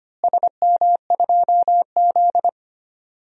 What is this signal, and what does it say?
Morse code, heard as a keyed tone, about 25 wpm, 700 Hz. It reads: SM2Z
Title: DXCC-100 Call Signs - 4 Characters - Repeated 3x - 25wpm